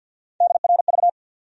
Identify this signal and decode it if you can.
Morse code, heard as a keyed tone, about 52 wpm, 700 Hz. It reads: DRV
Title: USA Call Sign Postfixes - 3 Letters 50wpm